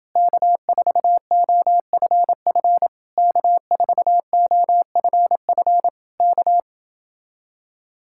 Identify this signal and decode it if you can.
Morse code, heard as a keyed tone, about 29 wpm, 700 Hz. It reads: K4OFF K4OFF K